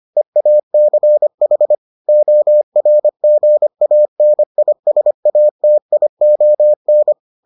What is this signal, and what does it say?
Morse code, heard as a keyed tone, about 25 wpm, 600 Hz. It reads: EACH ORGANISATION